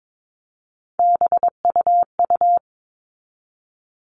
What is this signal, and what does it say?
Morse code, heard as a keyed tone, about 22 wpm, 700 Hz. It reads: BUU
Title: All Cut Numbers - 3-Digit 22wpm